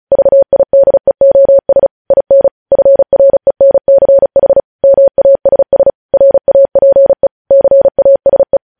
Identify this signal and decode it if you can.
Morse code, heard as a keyed tone, about 36 wpm, 550 Hz. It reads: VIDEOS IN FRENCH MASS RAPE CASE